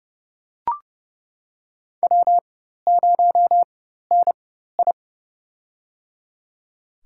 Morse code, heard as a keyed tone, about 30 wpm, spaced as for 14 wpm, 700 Hz. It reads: W0NI
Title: USA Call Signs - 4 Characters - 4x Character Spacing - 30wpm